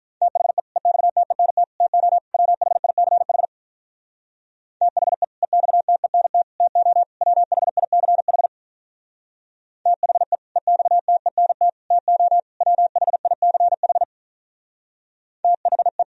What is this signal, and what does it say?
Morse code, heard as a keyed tone, about 44 wpm, 700 Hz. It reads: THE EXTENT TO WHICH THE EXTENT TO WHICH THE EXTENT TO WHICH THE